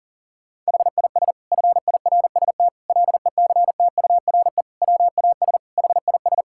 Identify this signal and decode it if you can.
Morse code, heard as a keyed tone, about 40 wpm, 700 Hz. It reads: HIS FIRST LECTURE WAS HIS
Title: Sentences from Top 1600 Words 40wpm